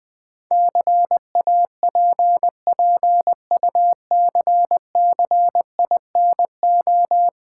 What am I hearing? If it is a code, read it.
Morse code, heard as a keyed tone, about 20 wpm, 690 Hz. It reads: CAPPUCCINO
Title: Easy Words with Hints - 20wpm